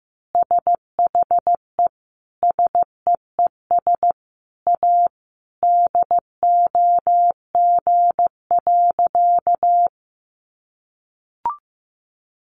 Morse code, heard as a keyed tone, about 15 wpm, 700 Hz. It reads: SHE SEES A DOG.
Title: Easy Phrases 15wpm